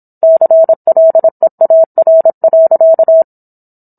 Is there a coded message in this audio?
Yes: Morse code from 0.2 to 3.2 s, heard as a keyed tone, about 27 wpm, 650 Hz. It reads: CLEAR.